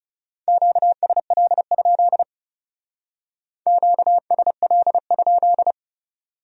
Morse code, heard as a keyed tone, about 38 wpm, 700 Hz. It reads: QSL? QSL?